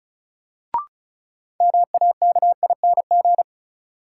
Morse code, heard as a keyed tone, about 36 wpm, 700 Hz. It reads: MAKING
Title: Sentences from Top 1000 Words - Review Part 6 - New Word Drill - 35wpm